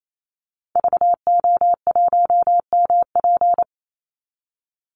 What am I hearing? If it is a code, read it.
Morse code, heard as a keyed tone, about 28 wpm, 700 Hz. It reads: VO1MP
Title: Call Signs Repeated 3x - 28wpm